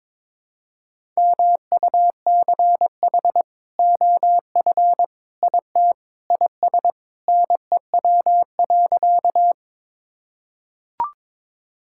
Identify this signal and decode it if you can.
Morse code, heard as a keyed tone, about 22 wpm, 700 Hz. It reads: MUCH OF IT IS NEW.
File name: practice / 22wpm / Sentences with 2-Word Phrases - 22wpm